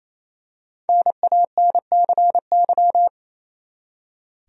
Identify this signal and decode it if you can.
Morse code, heard as a keyed tone, about 28 wpm, 700 Hz. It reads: NANCY
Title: USA Common Names - Max 5 Letters - 28wpm